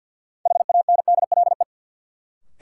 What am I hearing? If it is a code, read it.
Morse code, heard as a keyed tone, about 50 wpm, 700 Hz. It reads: HANDLE
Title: Sentences from Top 1100 Words - New Word Drill Alt - 50wpm